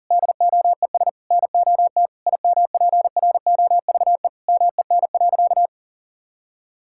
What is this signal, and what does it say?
Morse code, heard as a keyed tone, about 40 wpm, 700 Hz. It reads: DOES NOT IMPROVE MEN.
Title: Sentences from Top 1000 Words - Review Part 5 - 40wpm